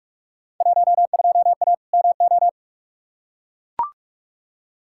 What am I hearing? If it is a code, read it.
Morse code, heard as a keyed tone, about 42 wpm, 700 Hz. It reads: 12A MO
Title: Field Day Exchanges Encore - 45wpm